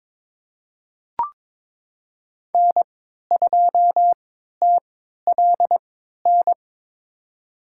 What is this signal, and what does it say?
Morse code, heard as a keyed tone, about 23 wpm, spaced as for 13 wpm, 700 Hz. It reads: N2TLN
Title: USA Call Signs - 5 Characters - 3x Character Spacing - 22wpm